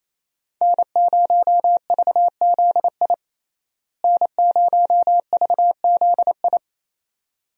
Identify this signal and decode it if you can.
Morse code, heard as a keyed tone, about 28 wpm, 700 Hz. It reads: N0VZI N0VZI
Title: USA Call Signs - 5 Characters - Repeated 2x - 28wpm